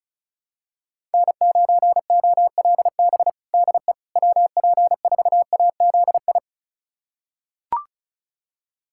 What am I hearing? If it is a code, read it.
Morse code, heard as a keyed tone, about 36 wpm, 690 Hz. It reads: N9OLB DE WP4AZI
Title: USA Callsign DE Callsign 35wpm